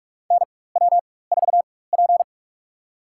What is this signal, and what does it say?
Morse code, heard as a keyed tone, about 46 wpm, spaced as for 21 wpm, 700 Hz. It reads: NW4P
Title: USA Call Signs - 4 Characters - 4x Character Spacing - 45wpm